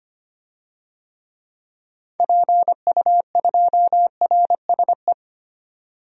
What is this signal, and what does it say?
Morse code, heard as a keyed tone, about 25 wpm, 700 Hz. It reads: PU2RSE
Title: DXCC-100 Call Signs - 6 Characters - 25wpm